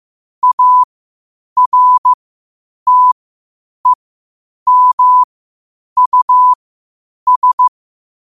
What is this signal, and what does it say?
Morse code, heard as a keyed tone, about 15 wpm, spaced as for 8 wpm, 1000 Hz. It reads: ARTEMUS
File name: qotdfiles / 8wpm